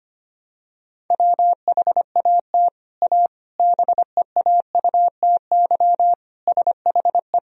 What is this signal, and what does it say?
Morse code, heard as a keyed tone, about 25 wpm, 700 Hz. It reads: WHAT A BEAUTY SHE
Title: Sentences from Top 1000 Words - Review Part 2 - 25wpm